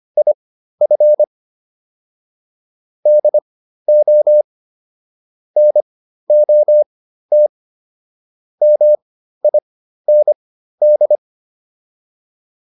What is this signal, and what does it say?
Morse code, heard as a keyed tone, about 25 wpm, spaced as for 13 wpm, 600 Hz. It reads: IF DO NOT MIND